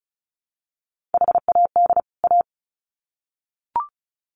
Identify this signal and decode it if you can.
Morse code, heard as a keyed tone, about 34 wpm, 700 Hz. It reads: HAD A
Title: Sets of 2 Words 35wpm